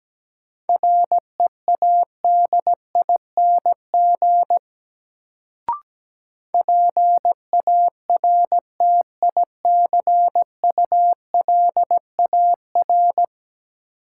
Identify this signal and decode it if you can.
Morse code, heard as a keyed tone, about 17 wpm, 700 Hz. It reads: READING PARTICULAR
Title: Sentences from Top 600 Words - New Word Drill - 17wpm